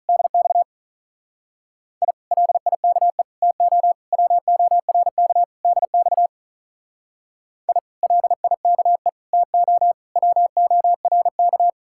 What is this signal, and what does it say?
Morse code, heard as a keyed tone, about 44 wpm, 700 Hz. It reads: DX I LIKE TO WORK DX I LIKE TO WORK
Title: Speed-Racing Common QSO Phrases - 35wpm